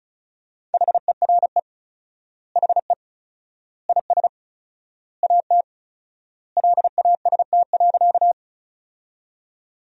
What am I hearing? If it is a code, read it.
Morse code, heard as a keyed tone, about 36 wpm, 700 Hz. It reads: HERE HE IS AT LAST.
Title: 4x Word-Spacing - Sentences from Top 300 Words - 35wpm